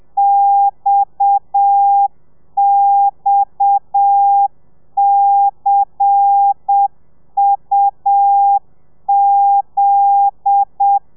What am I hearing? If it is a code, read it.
Morse code, heard as a keyed tone, about 7 wpm, 800 Hz. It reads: XXCUZ
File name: telegrafia / i2tfj / Lez10 30